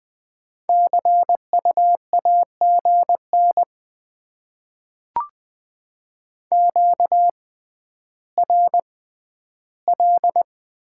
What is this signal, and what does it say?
Morse code, heard as a keyed tone, about 20 wpm, spaced as for 11 wpm, 690 Hz. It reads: CU AGN QRL
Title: Top 20 QSO Elements - 6x Character Spacing - 20wpm